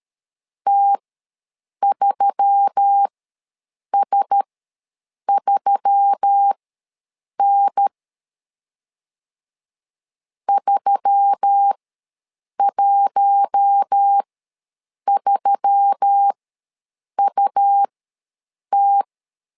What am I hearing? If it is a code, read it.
Morse code, heard as a keyed tone, about 13 wpm, spaced as for 7 wpm, 800 Hz. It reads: T3S3N 313UT
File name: LEZIONI / 4 Q1S3 ws 12wpm 2038